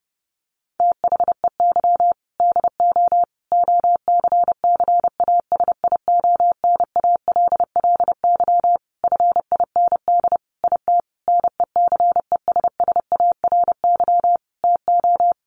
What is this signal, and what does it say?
Morse code, heard as a keyed tone, about 28 wpm, 700 Hz. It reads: THEY DO OCCASIONALLY FIND IT NECESSARY TO
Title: Sentences from Top 1300 Words 30wpm